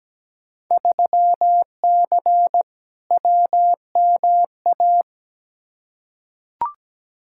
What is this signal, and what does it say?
Morse code, heard as a keyed tone, about 17 wpm, 700 Hz. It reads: 3C WMA
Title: Field Day Exchanges Encore - 17wpm